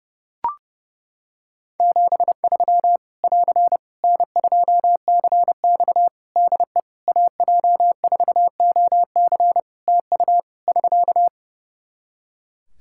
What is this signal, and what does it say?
Morse code, heard as a keyed tone, about 30 wpm, 700 Hz. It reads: 73 <AR> N2CX DE AJ4OC TU <SK>
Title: Practice QSO - Exchange 4 - Ending - 30wpm